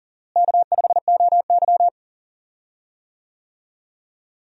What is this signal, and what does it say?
Morse code, heard as a keyed tone, about 40 wpm, 700 Hz. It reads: K5OY
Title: USA Call Signs - 4 Characters - Repeated 2x - 40wpm